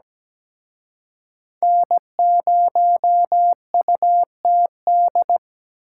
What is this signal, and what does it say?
Morse code, heard as a keyed tone, about 20 wpm, 700 Hz. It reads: N0UTD
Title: Speed Racing - USA 1x3 Call Signs 17wpm